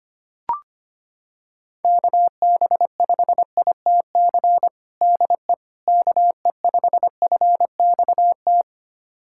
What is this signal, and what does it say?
Morse code, heard as a keyed tone, about 25 wpm, 700 Hz. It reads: KB5ITC DE KE5FXT
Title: USA Callsign DE Callsign - 6 Characters - 25wpm